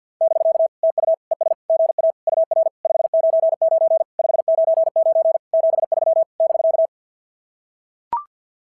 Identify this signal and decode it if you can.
Morse code, heard as a keyed tone, about 45 wpm, 690 Hz. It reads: <BK> TU ES GA UR 599 599 73 <BK>
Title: SOTA QSO Practice 50wpm